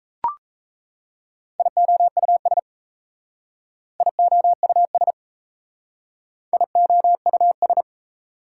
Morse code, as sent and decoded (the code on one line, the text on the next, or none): .. --- ..- ... / .. --- ..- ... / .. --- ..- ...
IOUS IOUS IOUS